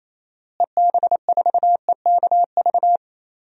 Morse code, heard as a keyed tone, about 28 wpm, 700 Hz. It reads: EB4EKV
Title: DXCC-100 Call Signs - 6 Characters - Repeated 3x - 28wpm